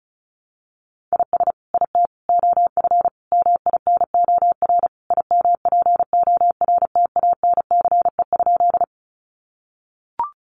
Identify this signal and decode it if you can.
Morse code, heard as a keyed tone, about 34 wpm, 700 Hz. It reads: IS IT OF MINOR IMPORTANCE?
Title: Sentences from Top 1500 Words 35wpm